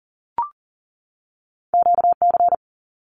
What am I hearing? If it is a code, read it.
Morse code, heard as a keyed tone, about 40 wpm, 700 Hz. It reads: QC
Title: Canadian Provinces and Territories 40wpm